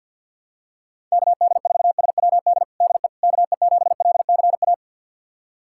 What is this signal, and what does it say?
Morse code, heard as a keyed tone, about 47 wpm, 690 Hz. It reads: KD4SWD DE KE7LCA
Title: USA Callsign DE Callsign - 6 Characters - 50wpm